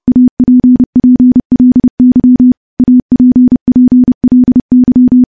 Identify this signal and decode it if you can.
Morse code, heard as a keyed tone, about 32 wpm, 250 Hz. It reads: APPLY APPLY
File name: morse-training-30WPM